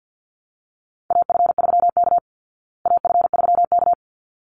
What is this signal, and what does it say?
Morse code, heard as a keyed tone, about 50 wpm, 700 Hz. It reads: AF3K AF3K
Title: Call Signs Repeated 3x - 50wpm